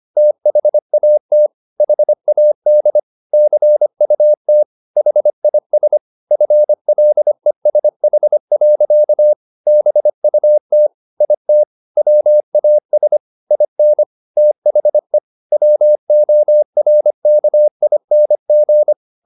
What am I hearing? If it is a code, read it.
Morse code, heard as a keyed tone, about 25 wpm, 600 Hz. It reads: THAT HAD CUT HIS FLESH. BUT IT WAS IN THE WORKING
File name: the old man and the sea part 1 25wpm